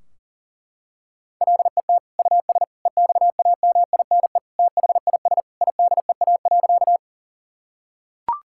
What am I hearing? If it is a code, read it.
Morse code, heard as a keyed tone, about 37 wpm, 700 Hz. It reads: LET US EXAMINE THIS IDEA.
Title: Sentences from Top 1600 Words 40wpm